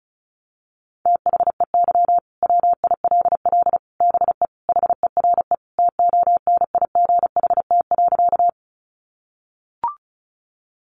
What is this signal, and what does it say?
Morse code, heard as a keyed tone, about 34 wpm, 700 Hz. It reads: THEY WILL BE HERE TONIGHT.
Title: Sentences from Top 600 Words 35wpm